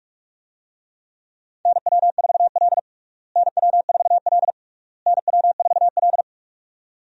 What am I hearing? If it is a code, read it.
Morse code, heard as a keyed tone, about 47 wpm, 700 Hz. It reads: NW4L NW4L NW4L
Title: USA Call Signs - 4 Characters - Repeated 3x - 45wpm